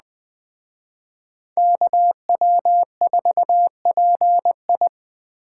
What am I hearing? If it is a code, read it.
Morse code, heard as a keyed tone, about 21 wpm, 700 Hz. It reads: KW4PI